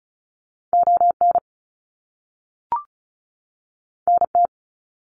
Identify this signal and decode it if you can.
Morse code, heard as a keyed tone, about 36 wpm, 700 Hz. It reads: ON NT
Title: Canadian Provinces and Territories 35wpm